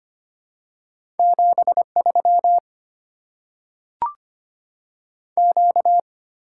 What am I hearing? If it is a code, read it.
Morse code, heard as a keyed tone, about 25 wpm, 700 Hz. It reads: 7 3 Q
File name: Top 90 QSO Elements - 6x Character Spacing - 25wpm